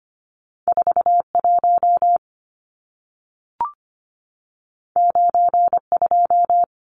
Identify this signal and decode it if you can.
Morse code, heard as a keyed tone, about 25 wpm, 700 Hz. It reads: 41 92
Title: Numbers - 2-Digit 25wpm